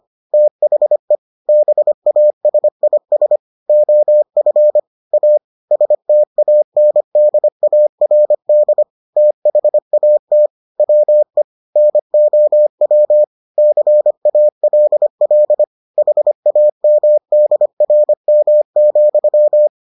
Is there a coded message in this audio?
Yes: Morse code from 0.3 to 19.7 s, heard as a keyed tone, about 25 wpm, 600 Hz. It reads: THE BASIS OF A STANDARD THAT WE NOW CALL HAMDRM,